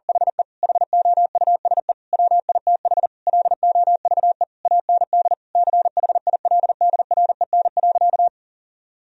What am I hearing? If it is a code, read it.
Morse code, heard as a keyed tone, about 40 wpm, 700 Hz. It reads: HE HOUSE WITH LOVE AND CHILDREN.
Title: Sentences from Top 1000 Words - Review Part 3 - 40wpm